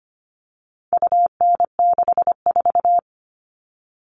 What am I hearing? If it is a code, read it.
Morse code, heard as a keyed tone, about 25 wpm, 700 Hz. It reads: UN64
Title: All Cut Numbers - 4-Digit 25wpm